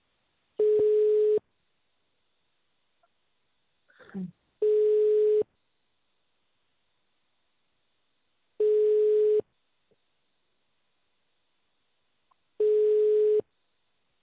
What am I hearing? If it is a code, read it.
Ringback tone (425 Hz, 1 s on, 3 s off).